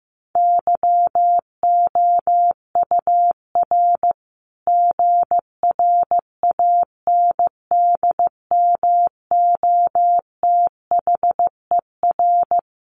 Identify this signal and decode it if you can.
Morse code, heard as a keyed tone, about 15 wpm, 700 Hz. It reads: YOUR GRANDMOTHER
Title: Sentences from Top 1200 Words 15wpm